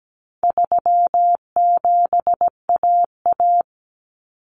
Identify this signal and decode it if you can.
Morse code, heard as a keyed tone, about 17 wpm, 700 Hz. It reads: 37AA
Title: Common Cut Numbers - 4-Digit 17wpm